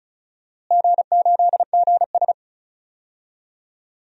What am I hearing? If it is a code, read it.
Morse code, heard as a keyed tone, about 36 wpm, 700 Hz. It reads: G8GS
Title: DXCC-100 Call Signs - 4 Characters - Repeated 3x - 35wpm